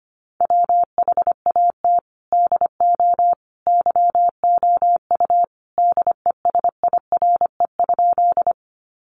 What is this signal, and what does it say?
Morse code, heard as a keyed tone, about 25 wpm, 700 Hz. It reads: WHAT DO YOU DESIRE?
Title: Sentences from Top 1100 Words 25wpm